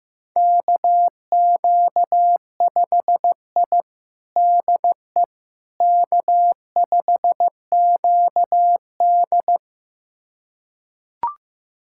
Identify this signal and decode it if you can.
Morse code, heard as a keyed tone, about 16 wpm, 700 Hz. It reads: KQ5I DE K5QD
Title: USA Callsign DE Callsign - 4 Characters - 15wpm